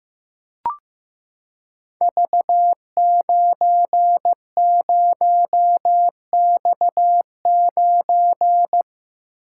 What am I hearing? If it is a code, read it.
Morse code, heard as a keyed tone, about 15 wpm, 700 Hz. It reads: V90X9
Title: Code Groups - 5 Letter:Numbers - 15wpm